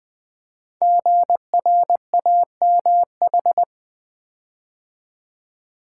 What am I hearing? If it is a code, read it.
Morse code, heard as a keyed tone, about 20 wpm, 700 Hz. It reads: GRAMH
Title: Code Groups - 5 Letters - 20wpm